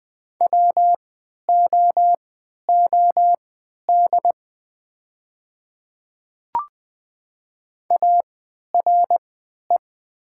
Morse code, heard as a keyed tone, about 20 wpm, spaced as for 11 wpm, 700 Hz. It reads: WOOD ARE
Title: 3x Character Spacing - Top 300 Words - 20wpm